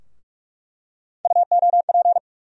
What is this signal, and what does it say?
Morse code, heard as a keyed tone, about 42 wpm, 700 Hz. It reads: UOP
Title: USA Call Sign Postfixes - 3 Letters 45wpm